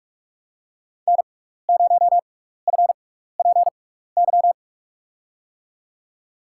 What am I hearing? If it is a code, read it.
Morse code, heard as a keyed tone, about 46 wpm, spaced as for 15 wpm, 700 Hz. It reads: N0FPY
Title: USA Call Signs - 5 Characters - 6x Character Spacing - 45wpm